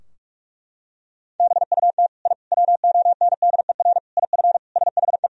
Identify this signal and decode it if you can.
Morse code, heard as a keyed tone, about 42 wpm, 700 Hz. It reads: BUT I WONDER IF SHE
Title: Sentences from Top 1000 Words - Review Part 3 - 45wpm